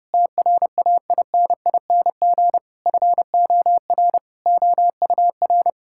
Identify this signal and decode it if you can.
Morse code, heard as a keyed tone, about 30 wpm, 700 Hz. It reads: TRAINING FOR OUR
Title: Sentences from Top 1000 Words - Review Part 5 - 30wpm